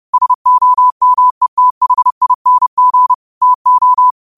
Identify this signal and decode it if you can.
Morse code, heard as a keyed tone, about 30 wpm, 1000 Hz. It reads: SOMETHING TO
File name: qotdfiles / 30wpm